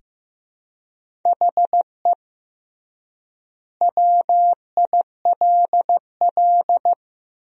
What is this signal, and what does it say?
Morse code, heard as a keyed tone, about 15 wpm, 700 Hz. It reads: HE WILL
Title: 3x Word-Spacing - Sentences from Top 500 Words - 15wpm